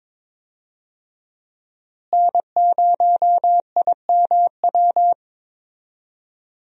Morse code, heard as a keyed tone, about 22 wpm, 700 Hz. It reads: N0IMW
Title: USA Call Signs - 5 Characters - Repeated 2x - 22wpm